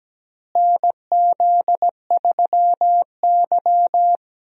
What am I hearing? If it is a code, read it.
Morse code, heard as a keyed tone, about 17 wpm, 700 Hz. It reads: NZ3Y
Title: USA Call Signs - 4 Characters - Repeated 2x - 17wpm